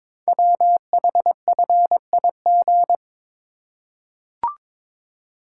Morse code, heard as a keyed tone, about 22 wpm, 700 Hz. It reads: WHFIG
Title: Code Groups - 5 Letters - 22wpm